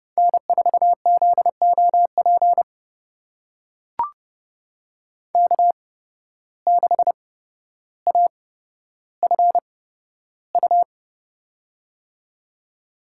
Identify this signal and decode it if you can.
Morse code, heard as a keyed tone, about 30 wpm, spaced as for 13 wpm, 700 Hz. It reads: N4ZOP K6AFU
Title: USA Call Signs - 5 Characters - 8x Character Spacing - 30wpm